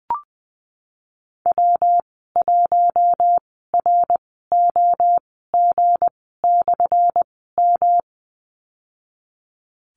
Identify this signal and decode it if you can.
Morse code, heard as a keyed tone, about 20 wpm, spaced as for 15 wpm, 700 Hz. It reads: W1ROG/M
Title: 179 Lesson 29g - Slash - Call Signs 20wpm